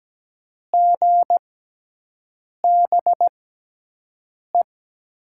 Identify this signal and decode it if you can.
Morse code, heard as a keyed tone, about 17 wpm, spaced as for 6 wpm, 700 Hz. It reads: GBE
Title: USA Call Signs - 6 Characters - 6x Character Spacing - 17wpm